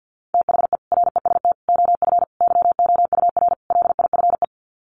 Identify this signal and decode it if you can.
Morse code, heard as a keyed tone, about 50 wpm, 700 Hz. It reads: THE REST OF YOUR LIFE
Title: Sets of 5 Words 50wpm